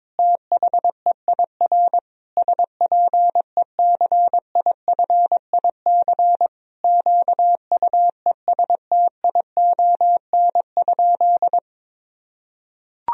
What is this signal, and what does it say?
Morse code, heard as a keyed tone, about 22 wpm, 700 Hz. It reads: THEIR SPECIFIC QUESTION?
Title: Sentences from Top 1000 Words - Review Part 3 - 22wpm